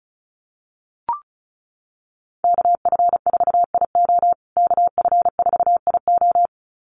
Morse code, heard as a keyed tone, about 34 wpm, 700 Hz. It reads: KF4IO KF4IO
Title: USA Call Signs Repeated 2x 35wpm